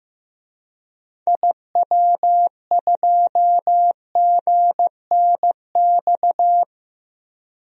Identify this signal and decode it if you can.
Morse code, heard as a keyed tone, about 15 wpm, 700 Hz. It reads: IW2GNX
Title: DXCC-100 Call Signs - 6 Characters - Repeated 3x - 15wpm